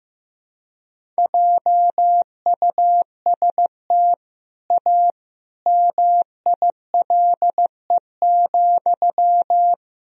Morse code, heard as a keyed tone, about 14 wpm, 700 Hz. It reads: JUST A MILE,
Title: Sentences from Top 1000 Words - Review Part 1 - 15wpm